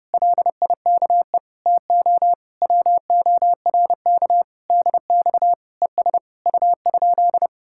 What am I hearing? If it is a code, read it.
Morse code, heard as a keyed tone, about 33 wpm, 700 Hz. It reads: LIKE TO WORK DX ES U?